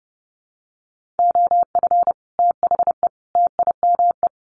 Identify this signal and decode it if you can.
Morse code, heard as a keyed tone, about 30 wpm, 700 Hz. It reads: OF THE TIME
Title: Sets of 3 Words 30wpm